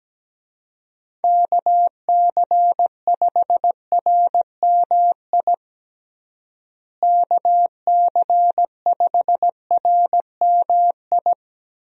Morse code, heard as a keyed tone, about 17 wpm, 700 Hz. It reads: KC5RMI KC5RMI
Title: USA Call Signs - 6 Characters - Repeated 3x - 17wpm